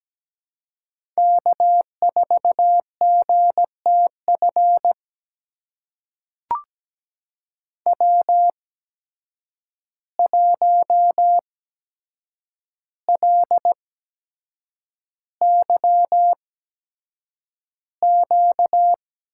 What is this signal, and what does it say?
Morse code, heard as a keyed tone, about 17 wpm, spaced as for 7 wpm, 700 Hz. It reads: K4GTF W1LYQ
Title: USA Call Signs - 5 Characters - 8x Character Spacing - 17wpm